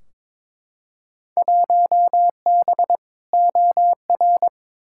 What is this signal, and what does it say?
Morse code, heard as a keyed tone, about 21 wpm, 700 Hz. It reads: 1B OR
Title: Field Day Exchanges Encore - 22wpm